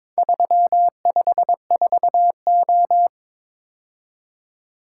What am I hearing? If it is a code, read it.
Morse code, heard as a keyed tone, about 22 wpm, 700 Hz. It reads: 354O
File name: practice / 22wpm / Four Letter-Number Combinations - 22wpm